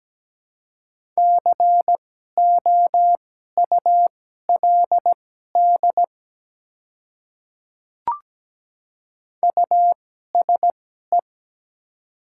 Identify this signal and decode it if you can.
Morse code, heard as a keyed tone, about 17 wpm, spaced as for 12 wpm, 690 Hz. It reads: COULD USE
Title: 2x Character Spacing - Top 100 Words - 17wpm